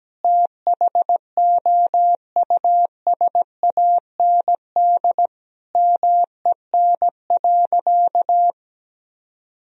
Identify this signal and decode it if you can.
Morse code, heard as a keyed tone, about 17 wpm, 700 Hz. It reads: THOUSAND MEN.
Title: Sentences from Top 1000 Words - Review Part 3 - 17wpm